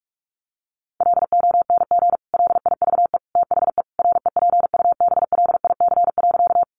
Morse code, heard as a keyed tone, about 44 wpm, 700 Hz. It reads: LONG LIVE THE REPUBLIC.
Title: Sentences from Top 1500 Words 45wpm